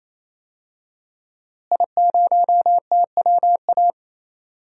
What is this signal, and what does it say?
Morse code, heard as a keyed tone, about 28 wpm, 700 Hz. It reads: I0TWA
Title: DXCC-100 Call Signs - 5 Characters - Repeated 3x - 28wpm